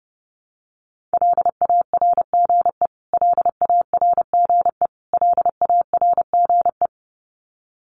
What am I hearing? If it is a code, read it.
Morse code, heard as a keyed tone, about 30 wpm, 700 Hz. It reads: LARGE LARGE LARGE
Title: Top 500 Words Repeated 30wpm